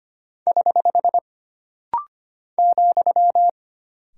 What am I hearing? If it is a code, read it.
Morse code, heard as a keyed tone, about 25 wpm, 700 Hz. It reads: <HH> ,
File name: Single Punctuation-Prosign - Rapid-Fire - 25wpm